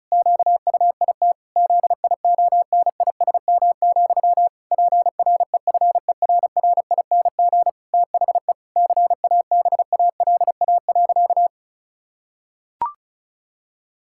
Morse code, heard as a keyed tone, about 36 wpm, 700 Hz. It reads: QUIT ZIONISM, PREFERRING THE CABALA.
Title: Pangram Challenge v2 - 35wpm